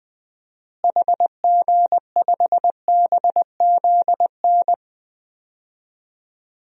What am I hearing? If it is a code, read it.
Morse code, heard as a keyed tone, about 20 wpm, 700 Hz. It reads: HG5BZN
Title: DXCC-100 Call Signs - 6 Characters - Repeated 3x - 20wpm